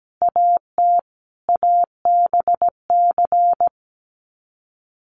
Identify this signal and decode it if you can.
Morse code, heard as a keyed tone, about 17 wpm, 700 Hz. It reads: AT ABC
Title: Sets of 5 Words 17wpm